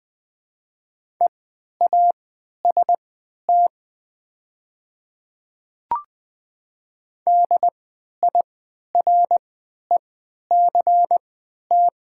Morse code, heard as a keyed tone, about 20 wpm, spaced as for 11 wpm, 700 Hz. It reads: EAST DIRECT
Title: 3x Character Spacing - Top 500 Words - 20wpm